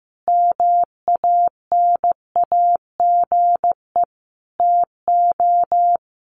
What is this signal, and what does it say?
Morse code, heard as a keyed tone, about 15 wpm, 700 Hz. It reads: MANAGE TO
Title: Sentences from Top 1400 Words 15wpm